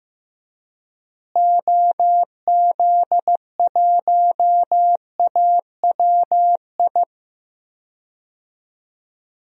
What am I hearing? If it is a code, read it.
Morse code, heard as a keyed tone, about 15 wpm, 700 Hz. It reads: OZ1AWI
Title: DXCC-100 Call Signs - 6 Characters - Repeated 3x - 15wpm